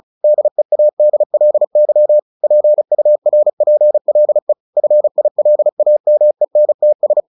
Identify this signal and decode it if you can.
Morse code, heard as a keyed tone, about 34 wpm, 600 Hz. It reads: DEADLY PURPLE FILAMENTS
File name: the old man and the sea / the old man and the sea 1 35wpm